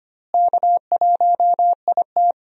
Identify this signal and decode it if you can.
Morse code, heard as a keyed tone, about 25 wpm, 700 Hz. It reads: K1IT
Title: USA Call Signs - 4 Characters - Repeated 2x - 25wpm